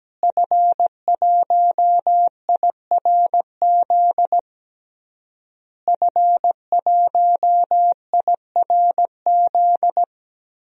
Morse code, heard as a keyed tone, about 17 wpm, 700 Hz. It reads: F1IRZ F1IRZ
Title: DXCC-100 Call Signs - 5 Characters - Repeated 3x - 17wpm